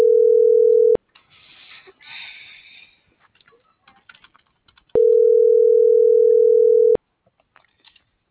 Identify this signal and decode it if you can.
Ringback tone (North America), 440 + 480 Hz.